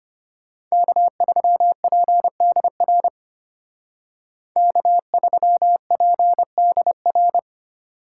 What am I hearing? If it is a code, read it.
Morse code, heard as a keyed tone, about 33 wpm, 700 Hz. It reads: K3PDR K3PDR